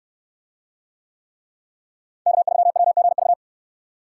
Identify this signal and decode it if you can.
Morse code, heard as a keyed tone, about 69 wpm, 700 Hz. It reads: K3WGV